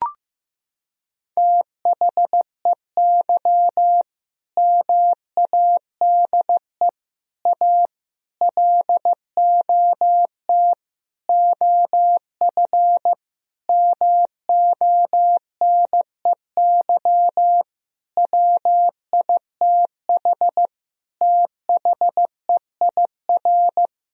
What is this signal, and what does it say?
Morse code, heard as a keyed tone, about 15 wpm, 700 Hz. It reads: THEY MADE A LOT OF MONEY WITH THEIR
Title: Sentences from Top 1000 Words - Review Part 4 - 15wpm